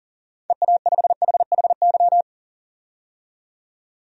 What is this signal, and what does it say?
Morse code, heard as a keyed tone, about 40 wpm, 700 Hz. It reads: EA5HHY
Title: DXCC-100 Call Signs - 6 Characters - 40wpm